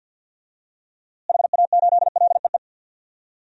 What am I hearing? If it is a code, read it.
Morse code, heard as a keyed tone, about 52 wpm, 700 Hz. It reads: HA8LEE
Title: DXCC-100 Call Signs - 6 Characters - Repeated 2x - 50wpm